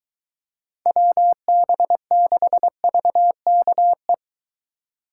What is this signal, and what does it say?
Morse code, heard as a keyed tone, about 22 wpm, 700 Hz. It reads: WB6VKE